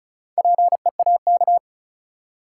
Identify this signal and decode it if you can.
Morse code, heard as a keyed tone, about 36 wpm, 700 Hz. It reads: PEAK
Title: Four Letter Words - 35wpm